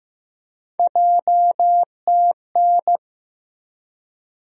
Morse code, heard as a keyed tone, about 15 wpm, 700 Hz. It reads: JTN
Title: USA Call Sign Postfixes - 3 Letters 15wpm